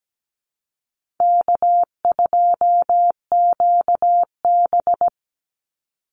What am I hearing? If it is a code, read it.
Morse code, heard as a keyed tone, about 17 wpm, 700 Hz. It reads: K2QB
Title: Easy Call Signs - 17wpm